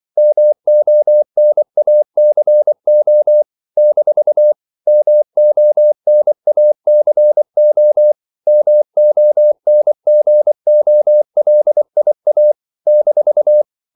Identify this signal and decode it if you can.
Morse code, heard as a keyed tone, about 24 wpm, 600 Hz. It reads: MONACO - MONACO MONGOLIA -
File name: CW / Practice / NationalCapitals24